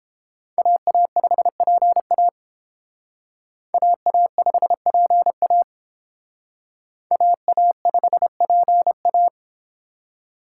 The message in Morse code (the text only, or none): AA5PA AA5PA AA5PA